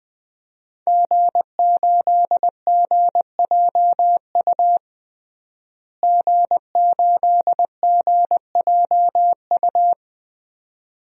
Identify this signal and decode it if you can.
Morse code, heard as a keyed tone, about 20 wpm, 700 Hz. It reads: G8GJU G8GJU
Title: DXCC-100 Call Signs - 5 Characters - Repeated 2x - 20wpm